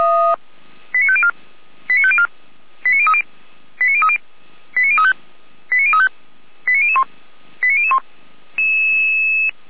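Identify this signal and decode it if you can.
Five-tone selective call (ZVEI), heard as quick runs of five tones, tones of about 70 ms. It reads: 89504 89504 89030 89030 89035 89035 89001 89001
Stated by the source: Received on an SDR by feuerwehr-orlamuende.org (fire brigade Orlamünde).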